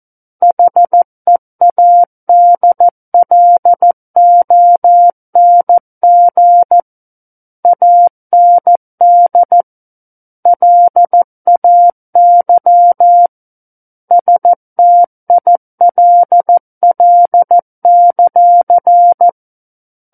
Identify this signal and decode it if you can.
Morse code, heard as a keyed tone, about 14 wpm, 700 Hz. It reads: HEADLONG AND LAY STILL;
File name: Book One / 05-Chapter 5 - 14 WPM